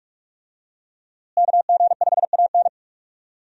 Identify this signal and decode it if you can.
Morse code, heard as a keyed tone, about 46 wpm, 700 Hz. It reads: KG5AN
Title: USA Call Signs - 5 Characters - Repeated 2x - 45wpm